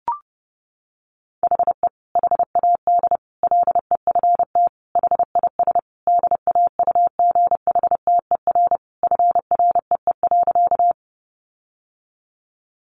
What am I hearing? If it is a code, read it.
Morse code, heard as a keyed tone, about 30 wpm, 700 Hz. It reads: HE HAD LEFT HIS DAUGHTER FREE.
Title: Sentences from Top 1500 Words 30wpm